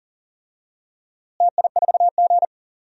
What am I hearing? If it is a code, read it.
Morse code, heard as a keyed tone, about 40 wpm, 700 Hz. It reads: TI4G
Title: DXCC-100 Call Signs - 4 Characters - Repeated 2x - 40wpm